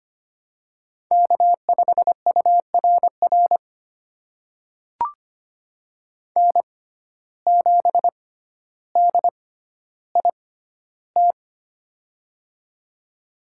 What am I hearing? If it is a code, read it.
Morse code, heard as a keyed tone, about 25 wpm, spaced as for 13 wpm, 700 Hz. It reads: K5URR N7DIT
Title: USA Call Signs - 5 Characters - 6x Character Spacing - 25wpm